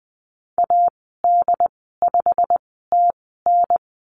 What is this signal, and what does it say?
Morse code, heard as a keyed tone, about 20 wpm, spaced as for 14 wpm, 700 Hz. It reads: AD5TN
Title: 088 Lesson 15g - Number 5 - Call Signs 20wpm